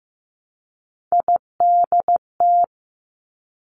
Morse code, heard as a keyed tone, about 15 wpm, 700 Hz. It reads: IDT
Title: Three Challenging Letter-Number Combinations 15wpm